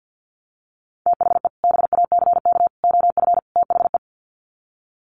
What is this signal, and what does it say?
Morse code, heard as a keyed tone, about 50 wpm, 700 Hz. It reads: THE BACK OF THE
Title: Sets of 4 Words 50wpm